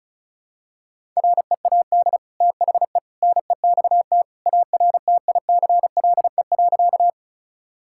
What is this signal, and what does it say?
Morse code, heard as a keyed tone, about 36 wpm, 700 Hz. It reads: READ THE NEXT ARTICLE.
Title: Sentences from Top 1000 Words - Review Part 5 - 35wpm